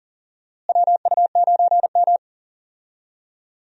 Morse code, heard as a keyed tone, about 40 wpm, 700 Hz. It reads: WU9M
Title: USA Call Signs - 4 Characters - Repeated 2x - 40wpm